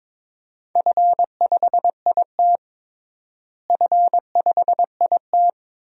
Morse code, heard as a keyed tone, about 22 wpm, 700 Hz. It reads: F5IT F5IT
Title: DXCC-100 Call Signs - 4 Characters - Repeated 2x - 22wpm